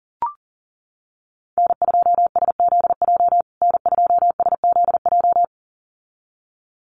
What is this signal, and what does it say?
Morse code, heard as a keyed tone, about 40 wpm, 700 Hz. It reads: N2SZJ N2SZJ
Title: USA Call Signs Repeated 2x 40wpm